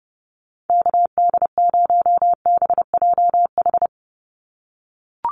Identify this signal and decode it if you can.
Morse code, heard as a keyed tone, about 30 wpm, 700 Hz. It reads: KD0BJH
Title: USA Call Signs 30wpm